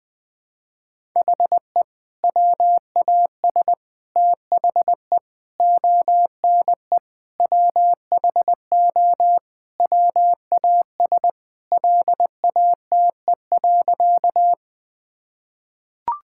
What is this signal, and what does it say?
Morse code, heard as a keyed tone, about 20 wpm, 700 Hz. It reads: HE WAS THE ONE WHO WAS LATE.
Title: Sentences with 2-Word Phrases - 20wpm